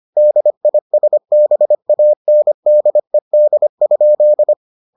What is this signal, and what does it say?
Morse code, heard as a keyed tone, about 25 wpm, 600 Hz. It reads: DISBANDED?